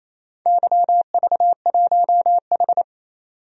Morse code, heard as a keyed tone, about 28 wpm, 700 Hz. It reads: YV1H
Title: DXCC-100 Call Signs - 4 Characters - 28wpm